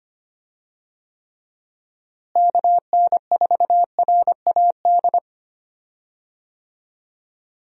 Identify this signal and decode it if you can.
Morse code, heard as a keyed tone, about 25 wpm, 700 Hz. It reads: KN4RAD
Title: USA Call Signs - 6 Characters - Repeated 2x - 25wpm